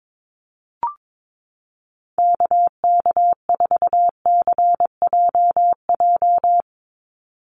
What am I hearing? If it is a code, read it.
Morse code, heard as a keyed tone, about 22 wpm, 700 Hz. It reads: KK4CJJ
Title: USA Call Signs 22wpm